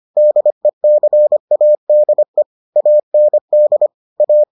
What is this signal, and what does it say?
Morse code, heard as a keyed tone, about 25 wpm, 600 Hz. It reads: DECADE AND A